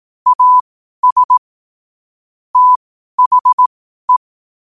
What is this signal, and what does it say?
Morse code, heard as a keyed tone, about 18 wpm, spaced as for 12 wpm, 1000 Hz. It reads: AS THE
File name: qotdfiles / 12wpm